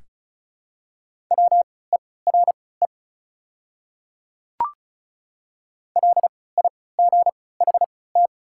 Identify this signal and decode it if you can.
Morse code, heard as a keyed tone, about 36 wpm, spaced as for 20 wpm, 700 Hz. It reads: WERE LIGHT
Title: 3x Character Spacing - Top 200 Words - 35wpm